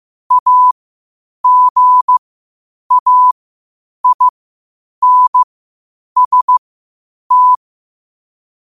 Morse code, heard as a keyed tone, about 15 wpm, spaced as for 8 wpm, 1000 Hz. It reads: AGAINST
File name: qotdfiles / 8wpm